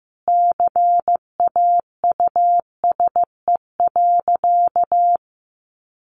Morse code, heard as a keyed tone, about 15 wpm, 700 Hz. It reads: CAUSE.
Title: Sentences from Top 1400 Words 15wpm